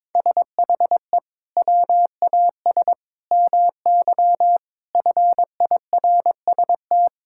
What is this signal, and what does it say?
Morse code, heard as a keyed tone, about 21 wpm, 700 Hz. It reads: SHE WAS MY FIRST
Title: Sentences from Top 1000 Words - Review Part 6 - 22wpm